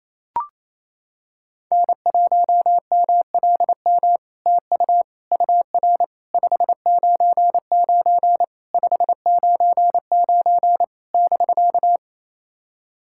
Morse code, heard as a keyed tone, about 28 wpm, 700 Hz. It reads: N1MLM TU UR 599 599 <BK>
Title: SOTA QSO Practice 28wpm